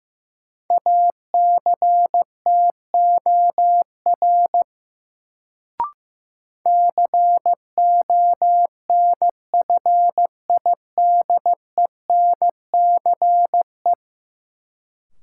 Morse code, heard as a keyed tone, about 15 wpm, 700 Hz. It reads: ACTOR CONFIDENCE
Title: Sentences from Top 1500 Words - New Word Drill Alt - 15wpm